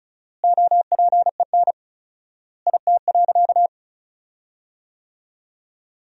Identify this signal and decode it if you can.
Morse code, heard as a keyed tone, about 34 wpm, 700 Hz. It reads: OPEN IT.
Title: 4x Word-Spacing - Sentences from Top 300 Words - 35wpm